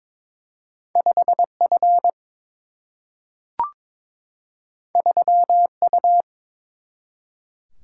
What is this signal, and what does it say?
Morse code, heard as a keyed tone, about 22 wpm, 700 Hz. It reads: 5F 3U
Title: Single-Letter Single-Number 22wpm